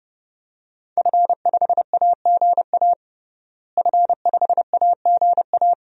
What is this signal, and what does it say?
Morse code, heard as a keyed tone, about 30 wpm, 700 Hz. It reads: F5AGA F5AGA
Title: DXCC-100 Call Signs - 5 Characters - Repeated 2x - 30wpm